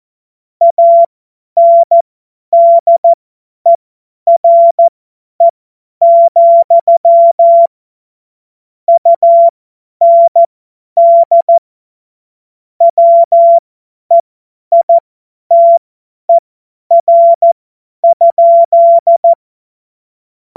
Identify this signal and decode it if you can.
Morse code, heard as a keyed tone, about 14 wpm, spaced as for 10 wpm, 680 Hz. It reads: ANDERE, UND WEITER?
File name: Morsekurs / Witze / Witz 20. Juli 2021, 50 F70